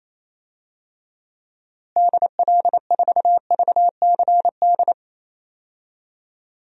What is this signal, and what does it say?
Morse code, heard as a keyed tone, about 28 wpm, 700 Hz. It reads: DL4VCD
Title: DXCC-100 Call Signs - 6 Characters - Repeated 2x - 28wpm